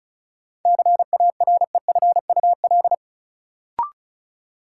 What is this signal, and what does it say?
Morse code, heard as a keyed tone, about 36 wpm, 700 Hz. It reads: CAREFUL
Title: Sentences from Top 1600 Words - New Word Drill Alt - 35wpm